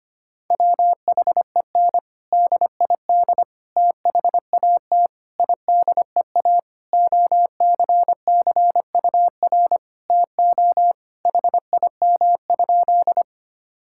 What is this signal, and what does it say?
Morse code, heard as a keyed tone, about 25 wpm, 700 Hz. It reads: WHEN DID THAT IDEA OCCUR TO HIM?
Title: Sentences from Top 1000 Words - Review Part 6 - 25wpm